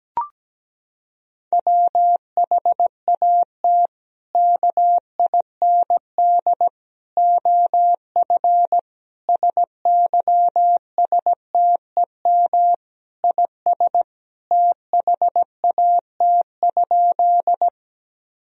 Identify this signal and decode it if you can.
Morse code, heard as a keyed tone, about 17 wpm, 700 Hz. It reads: WHAT KIND OF SYSTEM IS THAT?
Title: Sentences from Top 1000 Words - Review Part 3 - 17wpm